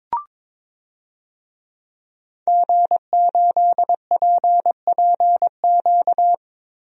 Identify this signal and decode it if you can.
Morse code, heard as a keyed tone, about 21 wpm, 700 Hz. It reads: G8PPQ
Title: DXCC-100 Call Signs - 5 Characters - Repeated 3x - 22wpm